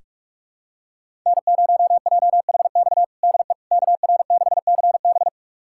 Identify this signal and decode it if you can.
Morse code, heard as a keyed tone, about 42 wpm, 700 Hz. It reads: N0JHX DE KR6CB
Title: USA Callsign DE Callsign - 5 Characters - 45wpm